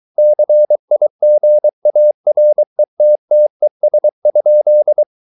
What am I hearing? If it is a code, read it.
Morse code, heard as a keyed tone, about 23 wpm, 600 Hz. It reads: CIGARETTES?